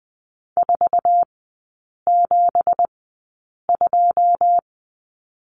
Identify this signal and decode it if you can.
Morse code, heard as a keyed tone, about 20 wpm, spaced as for 8 wpm, 700 Hz. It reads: 472
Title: 191 Lesson 31e - Letter v - Three Characters 20wpm